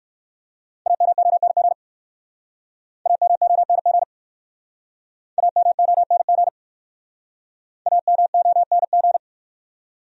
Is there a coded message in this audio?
Yes: Morse code from 0.9 to 9.2 s, heard as a keyed tone, about 64 wpm, 690 Hz. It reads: AMONG AMONG AMONG AMONG